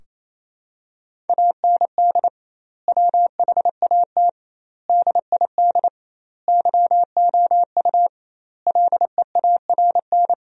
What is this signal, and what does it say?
Morse code, heard as a keyed tone, about 28 wpm, 700 Hz. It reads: AND WHAT DID YOU LEARN
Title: 2x Word-Spacing - Sentences from Top 300 Words - 28wpm